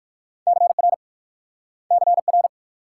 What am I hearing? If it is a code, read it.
Morse code, heard as a keyed tone, about 60 wpm, 700 Hz. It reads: CR CR